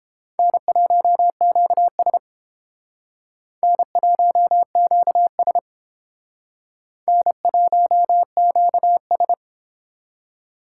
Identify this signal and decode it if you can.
Morse code, heard as a keyed tone, about 29 wpm, 700 Hz. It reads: N1QS N1QS N1QS